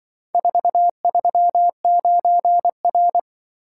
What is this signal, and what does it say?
Morse code, heard as a keyed tone, about 24 wpm, 700 Hz. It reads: 439R
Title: Speed-Racing - Four Letter-Number Combinations - 20wpm